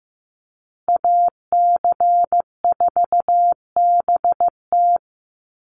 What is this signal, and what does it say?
Morse code, heard as a keyed tone, about 15 wpm, 700 Hz. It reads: AC4BT
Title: Call Signs Repeated 3x - 15wpm